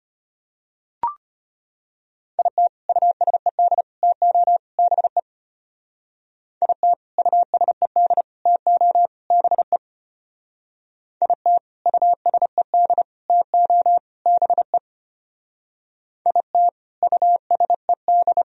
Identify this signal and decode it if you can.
Morse code, heard as a keyed tone, about 32 wpm, 700 Hz. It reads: IT USED TO BE IT USED TO BE IT USED TO BE IT USED